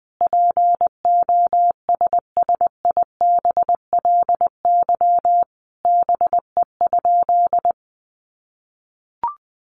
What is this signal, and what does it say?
Morse code, heard as a keyed tone, about 20 wpm, 700 Hz. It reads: POSSIBLY BE?
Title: Sentences from Top 1300 Words 20wpm